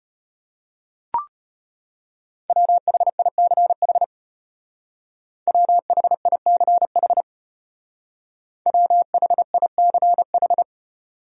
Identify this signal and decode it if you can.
Morse code, heard as a keyed tone, about 32 wpm, 700 Hz. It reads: WHICH WHICH WHICH